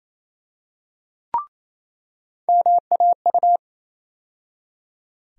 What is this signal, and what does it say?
Morse code, heard as a keyed tone, about 28 wpm, 700 Hz. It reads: MAU
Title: USA Call Sign Postfixes - 3 Letters 28wpm